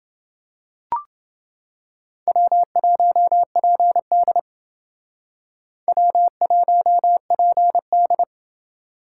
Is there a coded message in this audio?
Yes: Morse code from 2.3 to 8.2 s, heard as a keyed tone, about 28 wpm, 700 Hz. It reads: W1PD W1PD